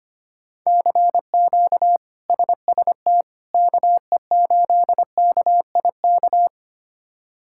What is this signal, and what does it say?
Morse code, heard as a keyed tone, about 25 wpm, 700 Hz. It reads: CQ SST KE8KIK
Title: K1USN SST QSO Practice - 25wpm